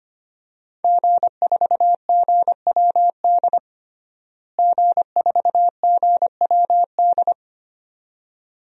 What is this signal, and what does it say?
Morse code, heard as a keyed tone, about 25 wpm, 700 Hz. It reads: G4GWD G4GWD
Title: DXCC-100 Call Signs - 5 Characters - Repeated 2x - 25wpm